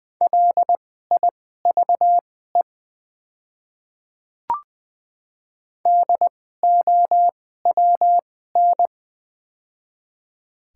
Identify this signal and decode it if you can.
Morse code, heard as a keyed tone, about 20 wpm, spaced as for 14 wpm, 700 Hz. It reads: LIVE DOWN
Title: 2x Character Spacing - Top 200 Words - 20wpm